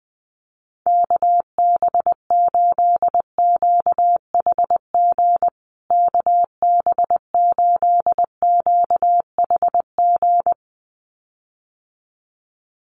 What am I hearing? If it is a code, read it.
Morse code, heard as a keyed tone, about 20 wpm, 700 Hz. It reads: KB8QHG KB8QHG
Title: USA Call Signs Repeated 2x 20wpm